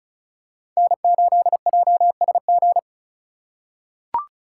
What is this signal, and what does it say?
Morse code, heard as a keyed tone, about 36 wpm, 700 Hz. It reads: N8JSG
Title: USA Call Signs - 5 Characters - 2x Character Spacing - 35wpm